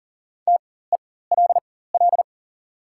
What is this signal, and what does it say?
Morse code, heard as a keyed tone, about 38 wpm, spaced as for 18 wpm, 690 Hz. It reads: TELL
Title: 4x Character Spacing - Top 500 Words - 40wpm